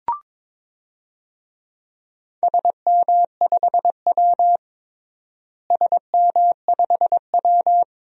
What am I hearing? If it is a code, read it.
Morse code, heard as a keyed tone, about 22 wpm, 700 Hz. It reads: SM5W SM5W
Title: DXCC-100 Call Signs - 4 Characters - Repeated 2x - 22wpm